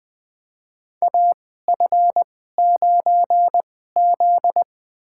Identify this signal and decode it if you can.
Morse code, heard as a keyed tone, about 20 wpm, spaced as for 15 wpm, 700 Hz. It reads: AF9Z
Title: USA Call Signs - 4 Characters - 2x Character Spacing - 20wpm